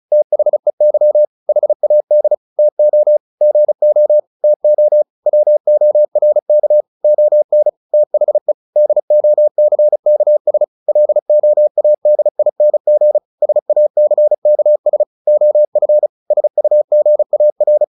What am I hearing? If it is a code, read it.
Morse code, heard as a keyed tone, about 34 wpm, 600 Hz. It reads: THEY HAD TO GO TO WORK ON THE DOCKS LOADING SACKS OF SUGAR
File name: the old man and the sea 2 35wpm